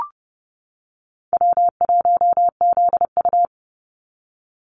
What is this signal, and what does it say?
Morse code, heard as a keyed tone, about 31 wpm, 700 Hz. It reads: W1ZU
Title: NA SOTA Activator Callsigns 30wpm